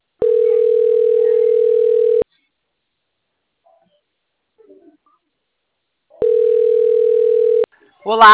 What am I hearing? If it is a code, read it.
Ringback tone (North America), 440 + 480 Hz, with speech.
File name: out- - - - - .75603